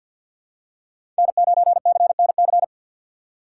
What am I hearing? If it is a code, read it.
Morse code, heard as a keyed tone, about 48 wpm, 700 Hz. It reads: N9CNC
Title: USA Call Signs - 5 Characters - 3x Character Spacing - 50wpm